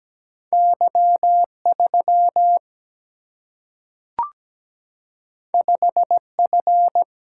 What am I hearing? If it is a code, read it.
Morse code, heard as a keyed tone, about 17 wpm, 700 Hz. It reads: Y 3 5 F
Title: Single-Letter Single-Number 17wpm